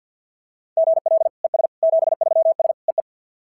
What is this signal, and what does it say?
Morse code, heard as a keyed tone, about 52 wpm, 690 Hz. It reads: GL ES 73S EE
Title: SOTA QSO Practice 50wpm